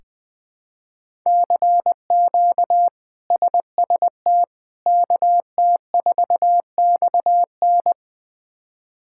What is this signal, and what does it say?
Morse code, heard as a keyed tone, about 20 wpm, 700 Hz. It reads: CQ SST KT4XN
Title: K1USN SST QSO Practice - 20wpm